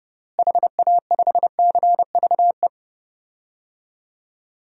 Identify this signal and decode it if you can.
Morse code, heard as a keyed tone, about 30 wpm, 700 Hz. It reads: HA5CVE
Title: DXCC-100 Call Signs - 6 Characters - Repeated 2x - 30wpm